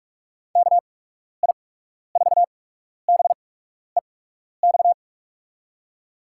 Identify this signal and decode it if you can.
Morse code, heard as a keyed tone, about 44 wpm, spaced as for 12 wpm, 700 Hz. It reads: KI4BEX
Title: USA Call Signs - 6 Characters - 8x Character Spacing - 45wpm